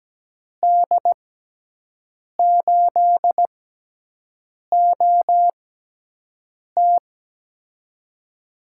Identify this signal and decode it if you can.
Morse code, heard as a keyed tone, about 17 wpm, spaced as for 6 wpm, 700 Hz. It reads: D8OT
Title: USA Call Signs - 5 Characters - 6x Character Spacing - 17wpm